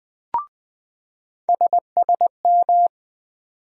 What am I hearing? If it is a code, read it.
Morse code, heard as a keyed tone, about 20 wpm, 700 Hz. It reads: SSM
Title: USA Call Sign Postfixes - 3 Letters 20wpm